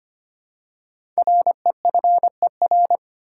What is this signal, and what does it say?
Morse code, heard as a keyed tone, about 25 wpm, 700 Hz. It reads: REFER
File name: 5 Letter Words - First 3 Letters Given - 25wpm